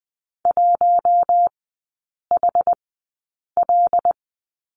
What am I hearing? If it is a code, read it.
Morse code, heard as a keyed tone, about 19 wpm, spaced as for 8 wpm, 700 Hz. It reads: 1HL
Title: 074 Lesson 13e - Letter L - Three Characters 20wpm